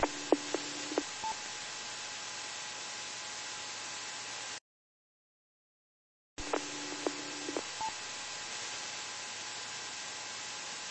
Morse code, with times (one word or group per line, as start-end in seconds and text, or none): none